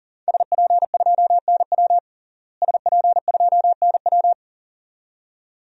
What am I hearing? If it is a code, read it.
Morse code, heard as a keyed tone, about 37 wpm, 700 Hz. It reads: SP2NW SP2NW
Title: DXCC-100 Call Signs - 5 Characters - Repeated 2x - 40wpm